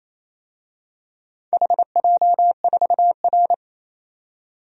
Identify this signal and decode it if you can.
Morse code, heard as a keyed tone, about 28 wpm, 700 Hz. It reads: HJ4R
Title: DXCC-100 Call Signs - 4 Characters - 28wpm